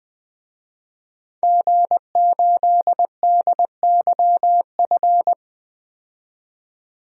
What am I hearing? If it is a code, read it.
Morse code, heard as a keyed tone, about 20 wpm, 700 Hz. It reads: G8DYF
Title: DXCC-100 Call Signs - 5 Characters - 20wpm